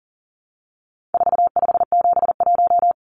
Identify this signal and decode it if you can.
Morse code, heard as a keyed tone, about 40 wpm, 700 Hz. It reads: 4571
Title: Numbers - 4-Digit 40wpm